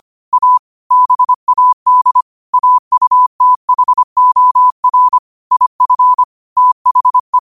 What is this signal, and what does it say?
Morse code, heard as a keyed tone, about 25 wpm, 1000 Hz. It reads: A BAD AUTHOR IF THE
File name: qotdfiles / 25wpm